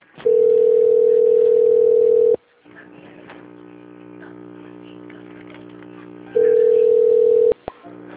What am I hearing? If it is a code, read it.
Ringback tone (North America), 440 + 480 Hz.